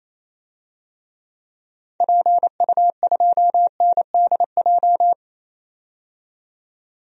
Morse code, heard as a keyed tone, about 28 wpm, 700 Hz. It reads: PU2NDJ
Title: DXCC-100 Call Signs - 6 Characters - 28wpm